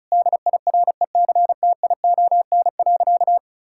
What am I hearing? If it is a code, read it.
Morse code, heard as a keyed tone, about 36 wpm, 690 Hz. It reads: DIRECTION.
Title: Sentences from Top 1000 Words - Review Part 5 - 35wpm